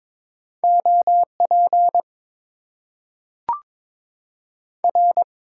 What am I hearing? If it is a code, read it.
Morse code, heard as a keyed tone, about 22 wpm, 700 Hz. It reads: OP R
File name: Top 90 QSO Elements - 8x Character Spacing - 22wpm